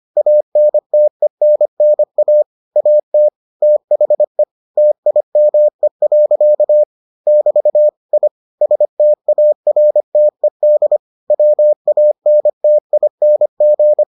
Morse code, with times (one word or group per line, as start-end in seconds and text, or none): 0.2-2.4 ANTENNA
2.8-3.3 AT
3.6-4.4 THE
4.8-6.8 TIME.
7.3-7.9 <BT>
8.1-8.3 I
8.6-11.0 STARTED
11.3-14.0 WANTING